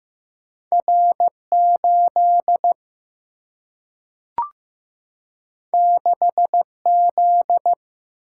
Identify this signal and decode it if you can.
Morse code, heard as a keyed tone, about 15 wpm, 700 Hz. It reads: R8 6Z
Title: Single-Letter Single-Number 15wpm